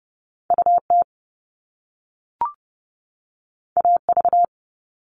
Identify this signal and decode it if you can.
Morse code, heard as a keyed tone, about 30 wpm, 700 Hz. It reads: UT AV
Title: Top Two Letter Combinations in Words 30wpm